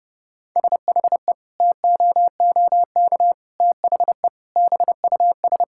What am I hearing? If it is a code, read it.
Morse code, heard as a keyed tone, about 30 wpm, 700 Hz. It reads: SHE TOOK THE BUS
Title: Sentences from Top 1000 Words - Review Part 6 - 30wpm